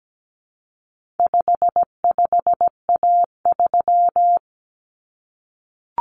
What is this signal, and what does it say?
Morse code, heard as a keyed tone, about 17 wpm, 700 Hz. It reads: 55A3
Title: Common Cut Numbers - 4-Digit 17wpm